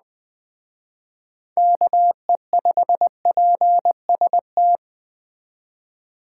Morse code, heard as a keyed tone, about 20 wpm, 700 Hz. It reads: KE5PST
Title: Speed Racing - USA Call Signs 17wpm